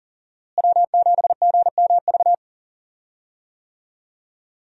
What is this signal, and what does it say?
Morse code, heard as a keyed tone, about 40 wpm, 700 Hz. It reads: W7GMV
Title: USA Call Signs - 5 Characters - Repeated 2x - 40wpm